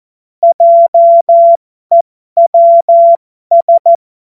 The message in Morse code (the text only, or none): JEWS